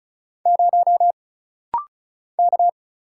Morse code, heard as a keyed tone, about 35 wpm, 700 Hz. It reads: 0 K
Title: Single Letter-Number - Rapid-Fire 35wpm